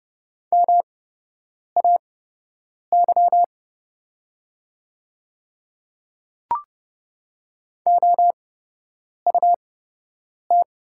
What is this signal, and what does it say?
Morse code, heard as a keyed tone, about 30 wpm, spaced as for 8 wpm, 700 Hz. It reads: MAY OUT
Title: 8x Character Spacing - Top 100 Words - 30wpm